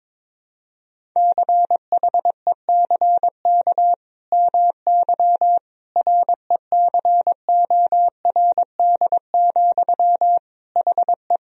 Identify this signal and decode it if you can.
Morse code, heard as a keyed tone, about 22 wpm, 700 Hz. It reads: CHECK MY RECORD, HE
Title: Sentences from Top 1000 Words - Review Part 3 - 22wpm